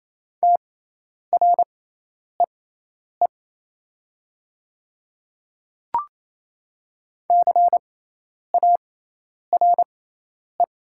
Morse code, heard as a keyed tone, about 28 wpm, spaced as for 10 wpm, 700 Hz. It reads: TREE CARE
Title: 6x Character Spacing - Top 500 Words - 28wpm